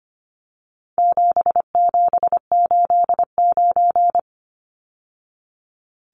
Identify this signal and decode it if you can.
Morse code, heard as a keyed tone, about 25 wpm, 700 Hz. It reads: 7789
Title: Numbers - 4-Digit 25wpm